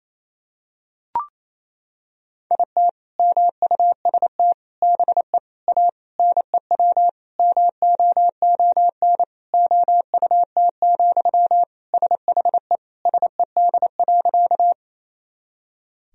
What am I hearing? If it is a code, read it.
Morse code, heard as a keyed tone, about 28 wpm, 700 Hz. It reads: IT MUST BE A NEW MOON OUT, SHE SED.
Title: Sentences from Top 500 Words and Abbreviations 28wpm